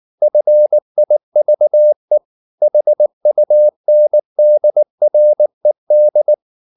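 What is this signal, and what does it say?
Morse code, heard as a keyed tone, about 19 wpm, 600 Hz. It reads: FIVE HUNDRED
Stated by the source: Morse code at 19 wpm from CW med Gnister, reading Ken Follett's novel Never in English.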